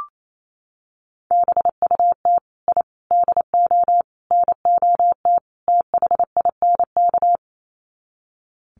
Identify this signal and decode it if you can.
Morse code, heard as a keyed tone, about 28 wpm, 700 Hz. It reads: BUT I DO NOT THINK
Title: Sets of 5 Words 28wpm